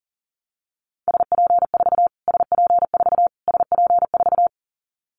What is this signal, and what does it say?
Morse code, heard as a keyed tone, about 40 wpm, 700 Hz. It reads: SP4 SP4 SP4
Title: Callsign Prefixes for Top 25 DXCC - Repeated 3x - 40wpm